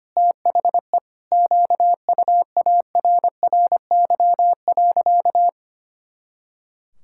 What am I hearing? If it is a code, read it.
Morse code, heard as a keyed tone, about 25 wpm, 700 Hz. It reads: THE QUARRY.
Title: Pangram Challenge v2 - 25wpm